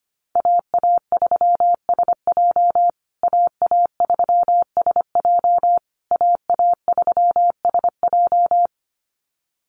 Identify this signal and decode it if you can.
Morse code, heard as a keyed tone, about 25 wpm, 700 Hz. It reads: AA3SJ AA3SJ AA3SJ
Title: USA Call Signs Repeated 3x 25wpm